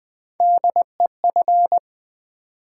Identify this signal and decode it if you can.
Morse code, heard as a keyed tone, about 20 wpm, 700 Hz. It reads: DEF
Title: Build A Word - 3 Letters At A Time - 6-Letter Words - 20wpm